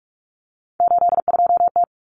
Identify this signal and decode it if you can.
Morse code, heard as a keyed tone, about 44 wpm, 700 Hz. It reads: 82T
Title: Common Cut Numbers - 3-Digit 45wpm